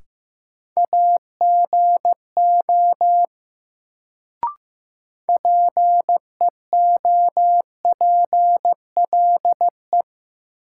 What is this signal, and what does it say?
Morse code, heard as a keyed tone, about 15 wpm, 700 Hz. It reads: AGO PEOPLE
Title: Sets of 2 Words - New Word Drill - 15wpm